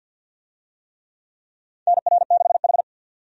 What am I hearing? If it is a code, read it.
Morse code, heard as a keyed tone, about 51 wpm, 700 Hz. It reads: NR6H
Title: USA Call Signs - 4 Characters - 50wpm